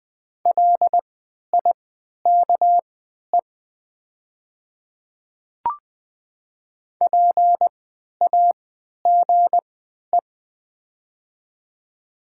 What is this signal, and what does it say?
Morse code, heard as a keyed tone, about 20 wpm, spaced as for 11 wpm, 690 Hz. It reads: LIKE PAGE
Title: 3x Character Spacing - Top 300 Words - 20wpm